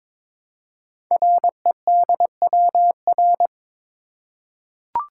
Morse code, heard as a keyed tone, about 23 wpm, 700 Hz. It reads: REDWR
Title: Code Groups - 5 Letters - 22wpm